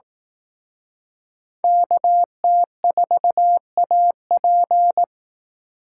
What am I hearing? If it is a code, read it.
Morse code, heard as a keyed tone, about 18 wpm, 700 Hz. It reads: KT4AP
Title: Speed Racing - USA 2x2 Call Signs 15wpm